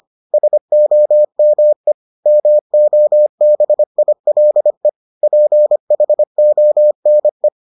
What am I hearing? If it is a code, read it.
Morse code, heard as a keyed tone, about 25 wpm, 600 Hz. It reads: SOME MOBILE PHONE